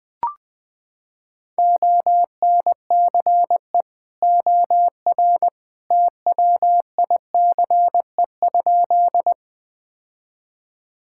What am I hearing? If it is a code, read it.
Morse code, heard as a keyed tone, about 20 wpm, 700 Hz. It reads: ONCE OR TWICE?
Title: Sentences from Top 1600 Words 20wpm